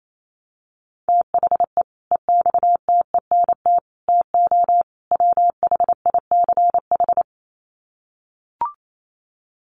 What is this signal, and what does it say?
Morse code, heard as a keyed tone, about 28 wpm, 700 Hz. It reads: THE EXTENT TO WHICH
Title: Sets of 4 Words 28wpm